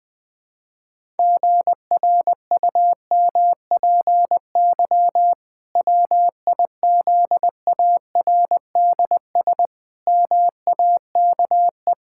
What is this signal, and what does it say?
Morse code, heard as a keyed tone, about 20 wpm, 700 Hz. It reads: GRUMPY WIZARDS MAKE
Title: Pangram Challenge v2 - 20wpm